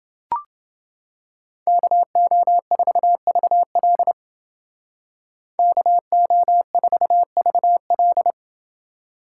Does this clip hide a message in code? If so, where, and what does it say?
Yes: Morse code from 1.7 to 8.3 s, heard as a keyed tone, about 28 wpm, 700 Hz. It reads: KO4VL KO4VL